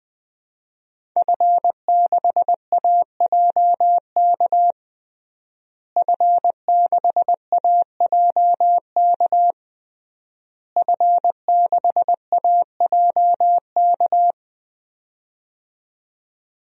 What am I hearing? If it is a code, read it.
Morse code, heard as a keyed tone, about 20 wpm, 700 Hz. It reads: F6AJK F6AJK F6AJK
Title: DXCC-100 Call Signs - 5 Characters - Repeated 3x - 20wpm